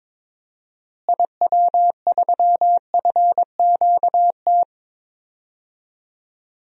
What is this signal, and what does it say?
Morse code, heard as a keyed tone, about 21 wpm, 700 Hz. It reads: IW3FQT
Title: DXCC-100 Call Signs - 6 Characters - Repeated 3x - 22wpm